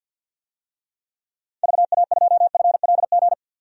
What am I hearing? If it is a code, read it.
Morse code, heard as a keyed tone, about 48 wpm, 700 Hz. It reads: VA2FLG
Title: DXCC-100 Call Signs - 6 Characters - 50wpm